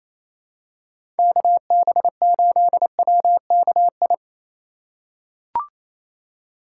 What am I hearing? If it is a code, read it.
Morse code, heard as a keyed tone, about 28 wpm, 700 Hz. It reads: KB8WKI
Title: USA Call Signs - 6 Characters - 3x Character Spacing - 28wpm